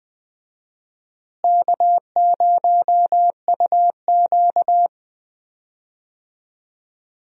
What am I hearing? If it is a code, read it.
Morse code, heard as a keyed tone, about 21 wpm, 700 Hz. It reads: K0UQ
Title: USA Call Signs - 4 Characters - Repeated 3x - 20wpm